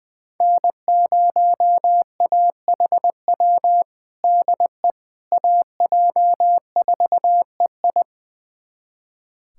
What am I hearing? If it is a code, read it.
Morse code, heard as a keyed tone, about 21 wpm, 700 Hz. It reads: N0AHW DE AJ4EI
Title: USA Callsign DE Callsign - 5 Characters - 20wpm